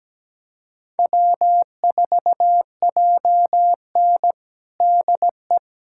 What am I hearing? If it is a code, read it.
Morse code, heard as a keyed tone, about 17 wpm, 700 Hz. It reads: W4JN DE
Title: USA Callsign DE Callsign - 4 Characters - 17wpm